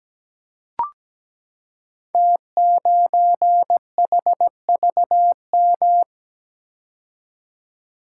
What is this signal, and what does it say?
Morse code, heard as a keyed tone, about 17 wpm, 700 Hz. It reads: T9HVM
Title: Code Groups - 5 Letter:Numbers - 17wpm